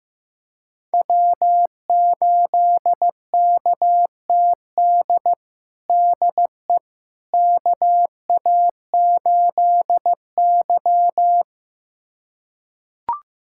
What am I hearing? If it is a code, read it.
Morse code, heard as a keyed tone, about 15 wpm, 700 Hz. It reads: W8KTD DE KA8Y
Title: USA Callsign DE Callsign 15wpm